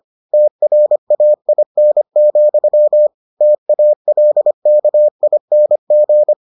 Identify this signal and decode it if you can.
Morse code, heard as a keyed tone, about 25 wpm, 600 Hz. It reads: TRAIN, TALKING